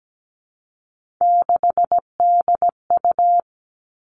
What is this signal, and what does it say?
Morse code, heard as a keyed tone, about 17 wpm, 700 Hz. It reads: 6DU
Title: Three Challenging Letter-Number Combinations 17wpm